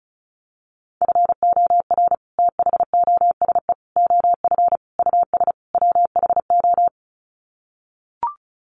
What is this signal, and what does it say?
Morse code, heard as a keyed tone, about 34 wpm, 700 Hz. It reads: FOR THOSE OF US WHO
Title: Sets of 5 Words Encore 35wpm